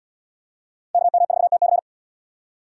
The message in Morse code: KM4TC

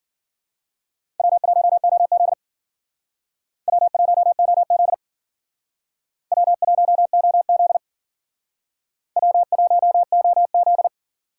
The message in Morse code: W1OZ W1OZ W1OZ W1OZ

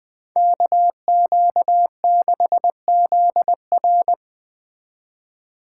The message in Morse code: KQ6ZR

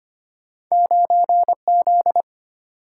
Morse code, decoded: 9Z

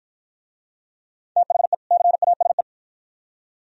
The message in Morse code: THE CASE

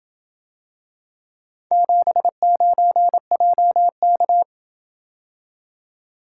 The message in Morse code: 79JK

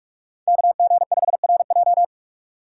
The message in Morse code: KG5RJ

